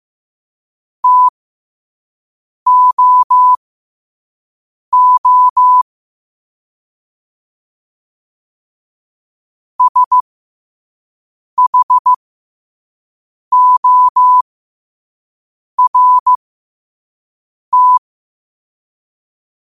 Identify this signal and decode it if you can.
Morse code, heard as a keyed tone, about 15 wpm, spaced as for 5 wpm, 1000 Hz. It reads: TOO SHORT